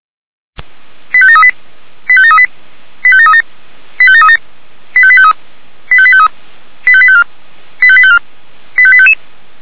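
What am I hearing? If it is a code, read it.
Five-tone selective call (ZVEI), heard as quick runs of five tones, tones of about 70 ms. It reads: 85738 85738 85737 85737 85853 85853 85854 85854 85855
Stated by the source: ZVEI five-tone selective call 85724 received by SDR.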